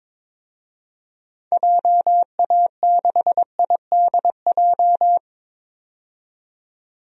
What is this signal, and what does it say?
Morse code, heard as a keyed tone, about 22 wpm, 700 Hz. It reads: JA6IDJ
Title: DXCC-100 Call Signs - 6 Characters - 22wpm